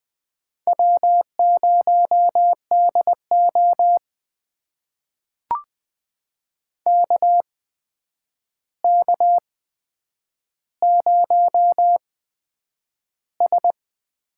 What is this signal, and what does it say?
Morse code, heard as a keyed tone, about 20 wpm, spaced as for 9 wpm, 700 Hz. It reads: W0DO KK0S
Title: USA Call Signs - 4 Characters - 8x Character Spacing - 20wpm